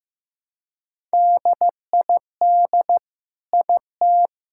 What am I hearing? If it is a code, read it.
Morse code, heard as a keyed tone, about 15 wpm, 700 Hz. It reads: DID IT
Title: Sentences from Top 1000 Words - Review Part 5 - 15wpm